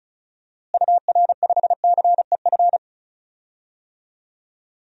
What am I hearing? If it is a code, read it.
Morse code, heard as a keyed tone, about 36 wpm, 700 Hz. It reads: UR5CEF
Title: DXCC-100 Call Signs - 6 Characters - Repeated 3x - 35wpm